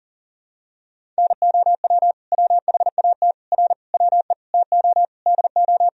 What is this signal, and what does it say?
Morse code, heard as a keyed tone, about 40 wpm, 700 Hz. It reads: NOW WHAT R WE TO DO
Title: Sentences from Top 100 Words and Abbreviations 40wpm